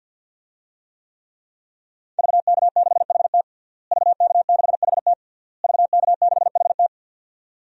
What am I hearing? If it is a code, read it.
Morse code, heard as a keyed tone, about 52 wpm, 700 Hz. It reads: VK6HT VK6HT VK6HT
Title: DXCC-100 Call Signs - 5 Characters - Repeated 3x - 50wpm